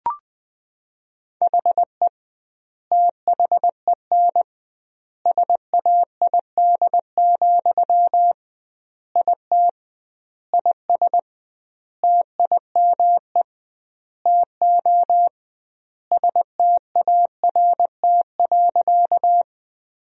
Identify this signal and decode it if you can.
Morse code, heard as a keyed tone, about 20 wpm, 700 Hz. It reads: HE THEN SAID, IT IS TIME TO START.
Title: 2x Word-Spacing - Sentences from Top 300 Words - 20wpm